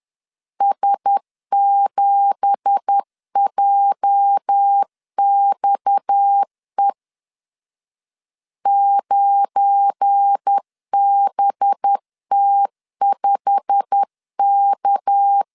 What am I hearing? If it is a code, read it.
Morse code, heard as a keyed tone, about 10 wpm, 800 Hz. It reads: S7JXE 9BT5K
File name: LEZIONI / 9 2867 10wpm 2733